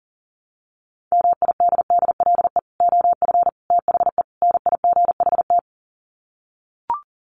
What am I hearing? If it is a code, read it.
Morse code, heard as a keyed tone, about 40 wpm, 700 Hz. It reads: MIDDLE OF THE NIGHT
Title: Sets of 4 Words 40wpm